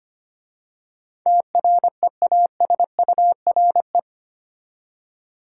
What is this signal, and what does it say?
Morse code, heard as a keyed tone, about 25 wpm, 700 Hz. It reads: TREASURE
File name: 8 Letter Words - First 3 Letters Given - 25wpm